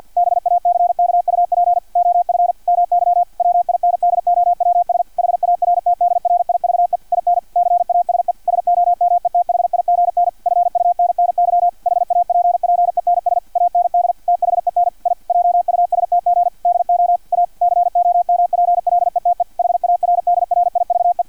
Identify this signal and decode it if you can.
Morse code, heard as a keyed tone, about 50 wpm, 700 Hz, with radio noise. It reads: BACKUP OF MY WINDOWS HARTDRIVE IN CASE SOMETHIGN FUNNY HAPPENS AND THEN I JUSTG DO A COMPLETE HARDRIVE
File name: archived / CRSnet